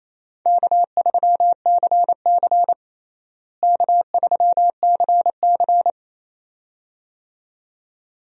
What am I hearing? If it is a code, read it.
Morse code, heard as a keyed tone, about 27 wpm, 700 Hz. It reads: K3CC K3CC
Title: USA Call Signs - 4 Characters - Repeated 2x - 28wpm